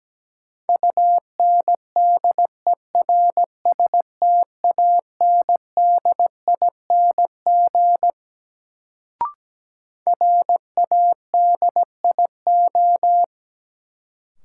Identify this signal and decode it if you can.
Morse code, heard as a keyed tone, about 17 wpm, 700 Hz. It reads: UNDERSTANDING RADIO
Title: Sentences from Top 1000 Words - Review Part 3 - New Word Drill - 17wpm